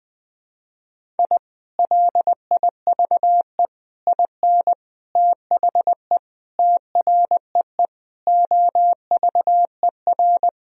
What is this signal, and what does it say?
Morse code, heard as a keyed tone, about 20 wpm, 700 Hz. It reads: I LIVE IN THE TREE OVER
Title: Sentences from Top 1000 Words - Review Part 1 - 20wpm